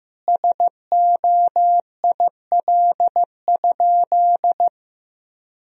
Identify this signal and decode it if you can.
Morse code, heard as a keyed tone, about 15 wpm, 700 Hz. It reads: SOIL?
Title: Sentences from Top 1000 Words - Review Part 4 - 15wpm